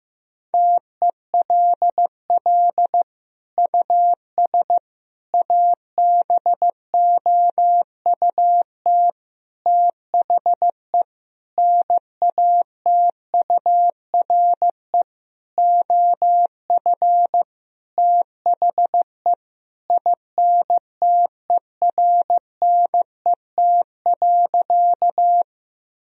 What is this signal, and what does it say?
Morse code, heard as a keyed tone, about 15 wpm, 700 Hz. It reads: TELL US ABOUT THE NATURE OF THE INTERNET.
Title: Sentences from Top 1000 Words - Review Part 4 - 15wpm